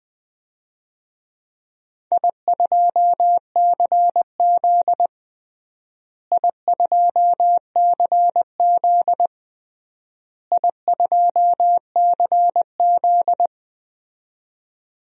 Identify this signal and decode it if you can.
Morse code, heard as a keyed tone, about 21 wpm, 700 Hz. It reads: I2CZ I2CZ I2CZ
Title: DXCC-100 Call Signs - 4 Characters - Repeated 3x - 20wpm